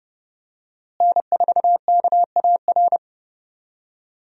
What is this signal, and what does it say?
Morse code, heard as a keyed tone, about 31 wpm, 700 Hz. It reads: N4KAR
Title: USA Call Signs - 5 Characters - Repeated 2x - 30wpm